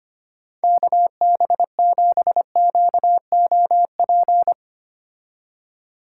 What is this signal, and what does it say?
Morse code, heard as a keyed tone, about 25 wpm, 700 Hz. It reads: KB7QOP
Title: K1USN SST QSO Practice - 25wpm